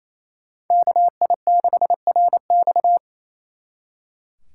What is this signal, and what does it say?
Morse code, heard as a keyed tone, about 32 wpm, 700 Hz. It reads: KI6RX
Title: Speed Racing - USA 2x2 Call Signs 28wpm